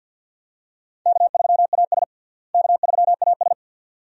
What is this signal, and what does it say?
Morse code, heard as a keyed tone, about 53 wpm, 700 Hz. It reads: K3AS K3AS
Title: USA Call Signs - 4 Characters - Repeated 2x - 50wpm